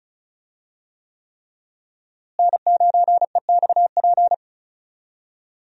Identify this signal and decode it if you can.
Morse code, heard as a keyed tone, about 34 wpm, 700 Hz. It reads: N9EXP
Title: USA Call Signs - 5 Characters - Repeated 3x - 35wpm